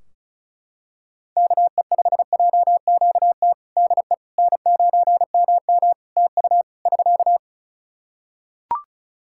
Morse code, heard as a keyed tone, about 36 wpm, 700 Hz. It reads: KE5JQT DE N9MM TU <SK>
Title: K1USN SST QSO Practice 35wpm